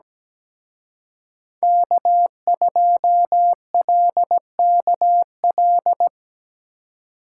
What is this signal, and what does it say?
Morse code, heard as a keyed tone, about 20 wpm, 700 Hz. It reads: K2LKL
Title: Speed Racing - USA 1x3 Call Signs 17wpm